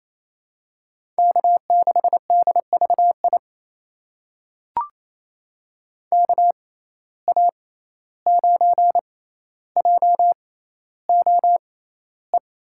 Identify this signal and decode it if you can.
Morse code, heard as a keyed tone, about 28 wpm, spaced as for 14 wpm, 700 Hz. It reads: K6DVI KA9JOE
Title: USA Call Signs - 6x Character Spacing - 28wpm